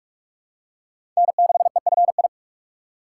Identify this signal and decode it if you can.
Morse code, heard as a keyed tone, about 41 wpm, 700 Hz. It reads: N6EFI
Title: USA Call Signs - 5 Characters - 2x Character Spacing - 45wpm